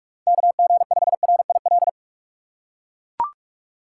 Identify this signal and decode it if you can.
Morse code, heard as a keyed tone, about 41 wpm, 700 Hz. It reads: KG5RIL
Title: USA Call Signs - 6 Characters - 3x Character Spacing - 45wpm